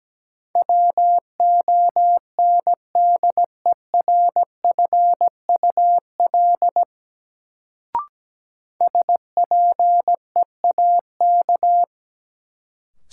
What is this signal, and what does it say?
Morse code, heard as a keyed tone, about 17 wpm, 700 Hz. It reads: WONDERFUL SPEAK
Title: Sentences from Top 1100 Words - New Word Drill Alt - 17wpm